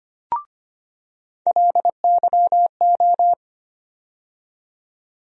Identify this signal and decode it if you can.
Morse code, heard as a keyed tone, about 25 wpm, 700 Hz. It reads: LYO
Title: USA Call Sign Postfixes - 3 Letters 25wpm